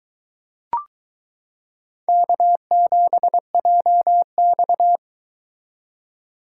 Morse code, heard as a keyed tone, about 22 wpm, 700 Hz. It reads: K7JX